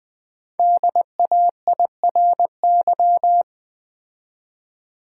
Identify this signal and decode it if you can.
Morse code, heard as a keyed tone, about 20 wpm, 700 Hz. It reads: DAIRY
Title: 5 Letter Words - First 3 Letters Given - 20wpm